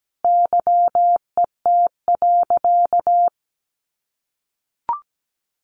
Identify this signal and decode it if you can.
Morse code, heard as a keyed tone, about 17 wpm, 700 Hz. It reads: YET.
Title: A Sound Of Thunder 17wpm